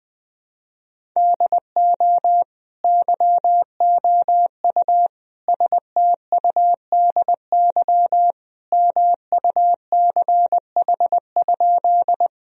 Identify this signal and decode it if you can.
Morse code, heard as a keyed tone, about 20 wpm, 700 Hz. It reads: DO YOU STUDY MUCH?
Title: Sentences from Top 1000 Words - Review Part 1 - 20wpm